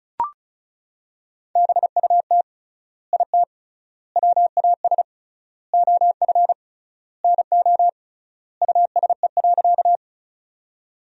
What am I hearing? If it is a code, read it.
Morse code, heard as a keyed tone, about 36 wpm, 700 Hz. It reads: BUT IT WAS OF NO USE.
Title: 3x Word-Spacing - Sentences from Top 100 Words - 35wpm